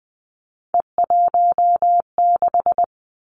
Morse code, heard as a keyed tone, about 20 wpm, 700 Hz. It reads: E16
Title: Three Challenging Letter-Number Combinations 20wpm